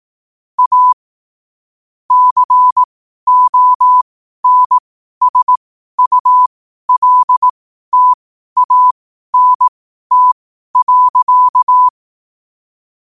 Morse code, heard as a keyed tone, about 18 wpm, spaced as for 12 wpm, 1000 Hz. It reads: A CONSULTANT.
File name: qotdfiles / 12wpm